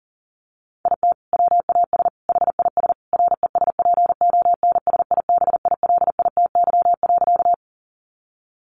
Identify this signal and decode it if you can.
Morse code, heard as a keyed tone, about 37 wpm, 700 Hz. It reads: IT WAS HIS RESPONSIBILITY.
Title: Sentences from Top 1200 Words 40wpm